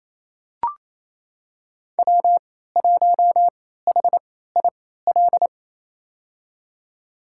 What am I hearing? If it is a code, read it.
Morse code, heard as a keyed tone, about 28 wpm, spaced as for 16 wpm, 700 Hz. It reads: W1HIL
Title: USA Call Signs - 3x Character Spacing - 28wpm